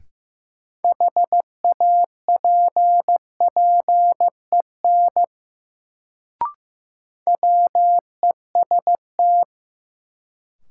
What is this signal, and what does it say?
Morse code, heard as a keyed tone, about 15 wpm, 700 Hz. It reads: HAPPEN WEST
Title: Sentences from Top 400 Words - New Word Drill - 15wpm